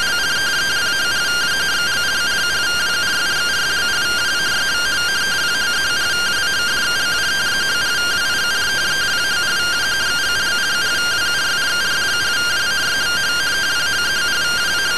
Two-tone teleprinter signal, about 170 Hz shift.